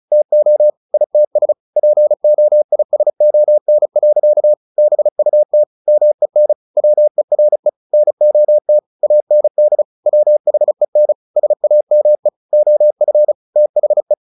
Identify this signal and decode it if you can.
Morse code, heard as a keyed tone, about 34 wpm, 600 Hz. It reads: TO ITS POISON. BUT MEN WERE NOT AND WHEN SAME OF THE
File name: the old man and the sea / the old man and the sea 1 35wpm